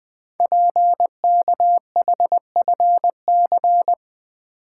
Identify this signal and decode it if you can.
Morse code, heard as a keyed tone, about 20 wpm, 700 Hz. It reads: PKHFC
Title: Code Groups - 5 Letters - 20wpm